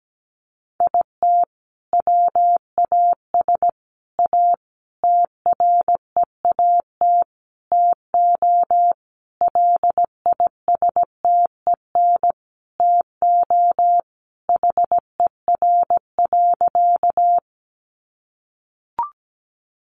Morse code, heard as a keyed tone, about 17 wpm, 700 Hz. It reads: IT WAS A TREAT TO LISTEN TO HER.
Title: Sentences from Top 1300 Words 17wpm